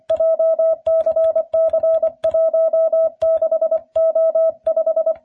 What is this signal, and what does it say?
Morse code, heard as a keyed tone, about 24 wpm, 650 Hz. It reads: JCC16O5